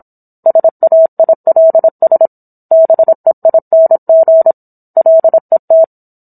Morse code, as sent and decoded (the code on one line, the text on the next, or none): ... .- .. .-.. ... / -... .. -. --. / .-.. -
SAILS BEING LET